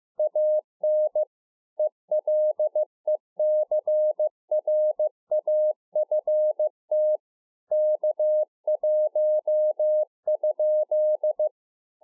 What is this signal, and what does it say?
Morse code, heard as a keyed tone, about 15 wpm, 610 Hz. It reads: AN ELECRAFT K1?
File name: 2011-04-mp3 / A SMALL TRANCEIVER 2 15WPM